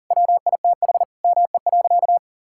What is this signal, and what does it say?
Morse code, heard as a keyed tone, about 40 wpm, 700 Hz. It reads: WITH ME.
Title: Sentences from Top 1000 Words - Review Part 6 - 40wpm